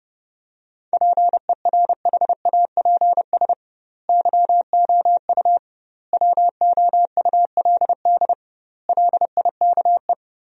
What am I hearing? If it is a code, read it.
Morse code, heard as a keyed tone, about 30 wpm, 700 Hz. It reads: PERHAPS YOU WOULD LIKE
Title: 2x Word-Spacing - Sentences from Top 500 Words - 30wpm